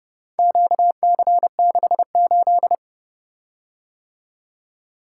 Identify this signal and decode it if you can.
Morse code, heard as a keyed tone, about 30 wpm, 700 Hz. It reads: QC68